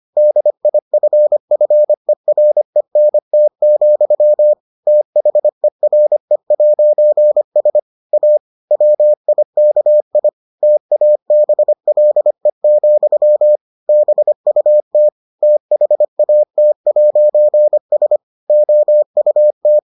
Morse code, heard as a keyed tone, about 25 wpm, 600 Hz. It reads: DIFFERENT, THERE'S A WIKI TABLE, BUT THAT'S OUT